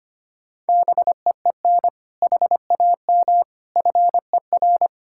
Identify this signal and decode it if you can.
Morse code, heard as a keyed tone, about 25 wpm, 700 Hz. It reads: BEEN HAM FER
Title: Practice QSO - Exchange 3 - More Info - 25wpm